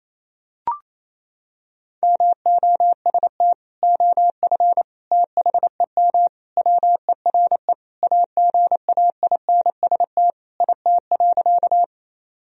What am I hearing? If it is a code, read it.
Morse code, heard as a keyed tone, about 28 wpm, 700 Hz. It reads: MOST OF THEM WERE AGAINST IT.
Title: Sentences from Top 1000 Words - Review Part 1 - 28wpm